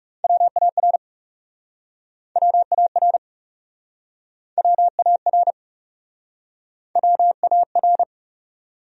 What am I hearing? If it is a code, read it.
Morse code, heard as a keyed tone, about 39 wpm, 700 Hz. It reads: WAR WAR WAR WAR